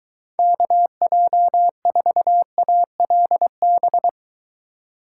Morse code, heard as a keyed tone, about 23 wpm, 700 Hz. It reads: KJ4ALB